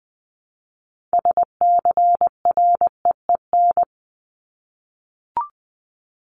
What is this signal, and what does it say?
Morse code, heard as a keyed tone, about 20 wpm, 700 Hz. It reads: SCREEN
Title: Top 751-1000 Words 20wpm